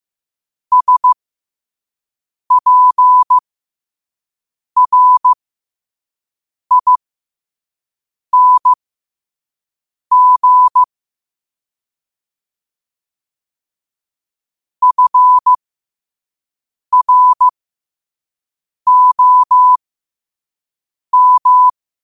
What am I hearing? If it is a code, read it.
Morse code, heard as a keyed tone, about 15 wpm, spaced as for 5 wpm, 1000 Hz. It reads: SPRING FROM
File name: qotdfiles / 5wpm